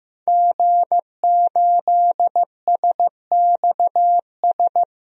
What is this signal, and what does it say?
Morse code, heard as a keyed tone, about 15 wpm, 700 Hz. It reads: G8SXS
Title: DXCC-100 Call Signs - 5 Characters - Repeated 3x - 15wpm